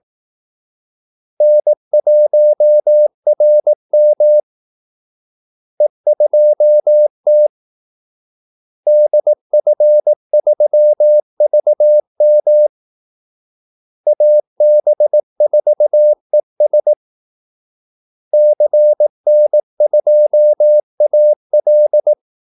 Morse code, heard as a keyed tone, about 18 wpm, 600 Hz. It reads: N1RM E2T DF3VM AB4ES CN2AL
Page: Kaldesignaler 18 wpm